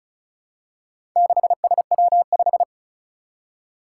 Morse code, heard as a keyed tone, about 36 wpm, 700 Hz. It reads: 6SW5
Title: Four Challenging Letter-Number Combinations 35wpm